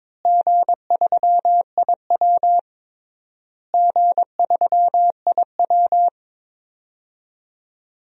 Morse code, heard as a keyed tone, about 22 wpm, 700 Hz. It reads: G3IW G3IW
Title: DXCC-100 Call Signs - 4 Characters - Repeated 3x - 22wpm